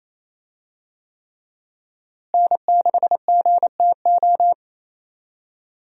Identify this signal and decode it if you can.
Morse code, heard as a keyed tone, about 28 wpm, 700 Hz. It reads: N6GTO
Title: USA Call Signs - 5 Characters - Repeated 3x - 28wpm